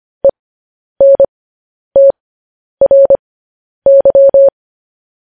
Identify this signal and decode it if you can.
Morse code, heard as a keyed tone, about 25 wpm, spaced as for 10 wpm, 550 Hz. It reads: ENTRY
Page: news headlines in morse code at 10 wpm